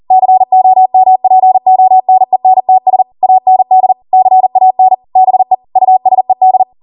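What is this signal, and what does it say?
Morse code, heard as a keyed tone, about 40 wpm, 750 Hz. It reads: COMPONENTS AND CAN BE USED
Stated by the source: ARRL 40 WPM Morse code practice transmission.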